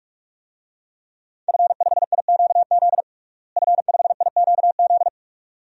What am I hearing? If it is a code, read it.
Morse code, heard as a keyed tone, about 46 wpm, 700 Hz. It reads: F5IQZ F5IQZ
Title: DXCC-100 Call Signs - 5 Characters - Repeated 2x - 45wpm